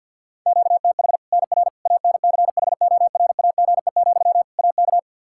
Morse code, heard as a keyed tone, about 46 wpm, 700 Hz. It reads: QTH NR ANCHORAGE, AK
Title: K1USN SST QSO Practice 50wpm